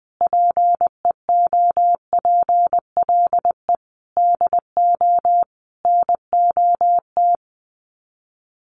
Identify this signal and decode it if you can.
Morse code, heard as a keyed tone, about 20 wpm, 700 Hz. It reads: PEOPLE DO NOT
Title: Sets of 5 Words Encore 20wpm